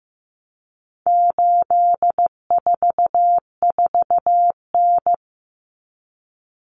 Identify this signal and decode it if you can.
Morse code, heard as a keyed tone, about 15 wpm, 700 Hz. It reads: 844N
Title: Common Cut Numbers - 4-Digit 15wpm